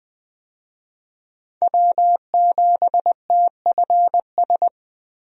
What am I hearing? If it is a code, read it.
Morse code, heard as a keyed tone, about 20 wpm, 700 Hz. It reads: W7TFS
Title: USA Call Signs - 5 Characters - 20wpm